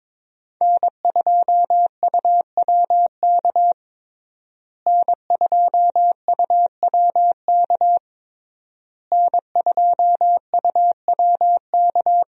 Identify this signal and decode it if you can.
Morse code, heard as a keyed tone, about 22 wpm, 700 Hz. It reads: N2UWK N2UWK N2UWK
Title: USA Call Signs - 5 Characters - Repeated 3x - 22wpm